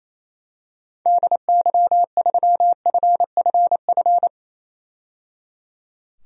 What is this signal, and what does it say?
Morse code, heard as a keyed tone, about 27 wpm, 700 Hz. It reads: DY3FFF
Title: DXCC-100 Call Signs - 6 Characters - 28wpm